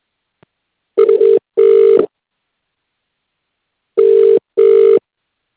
Ringback tone (UK), 400 + 450 Hz.